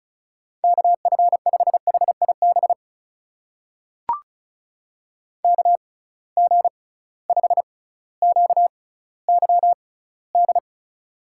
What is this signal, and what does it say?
Morse code, heard as a keyed tone, about 36 wpm, spaced as for 18 wpm, 700 Hz. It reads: KF5HIB KG5QYD
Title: USA Call Signs - 6 Characters - 6x Character Spacing - 35wpm